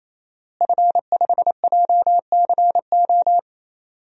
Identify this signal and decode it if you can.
Morse code, heard as a keyed tone, about 28 wpm, 700 Hz. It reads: F5JCO
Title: DXCC-100 Call Signs - 5 Characters - Repeated 3x - 28wpm